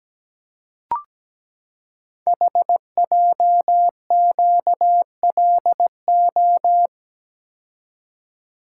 Morse code, heard as a keyed tone, about 17 wpm, 700 Hz. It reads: HJQLO
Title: Code Groups - 5 Letters - 17wpm